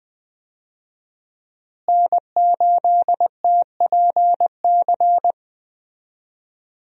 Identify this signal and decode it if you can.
Morse code, heard as a keyed tone, about 20 wpm, 700 Hz. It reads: N8TPC
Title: USA Call Signs - 5 Characters - Repeated 2x - 20wpm